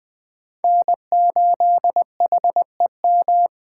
Morse code, heard as a keyed tone, about 21 wpm, 700 Hz. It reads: N8HEM
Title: Speed Racing - USA Call Signs 17wpm